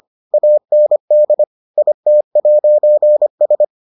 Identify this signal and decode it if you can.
Morse code, heard as a keyed tone, about 25 wpm, 600 Hz. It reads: AND IT'S